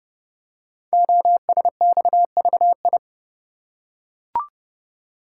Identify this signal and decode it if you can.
Morse code, heard as a keyed tone, about 30 wpm, 700 Hz. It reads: OSXVI
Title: Code Groups - 5 Letters - 30wpm